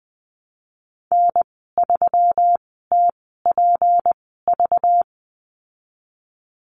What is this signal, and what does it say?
Morse code, heard as a keyed tone, about 20 wpm, spaced as for 15 wpm, 700 Hz. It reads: N3TPV
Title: 193 Lesson 31g - Letter v - Call Signs 20wpm